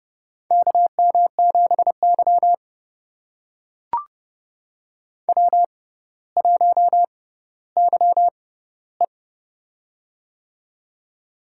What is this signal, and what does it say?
Morse code, heard as a keyed tone, about 29 wpm, spaced as for 15 wpm, 700 Hz. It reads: KM7Y W1YE
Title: USA Call Signs - 4 Characters - 6x Character Spacing - 30wpm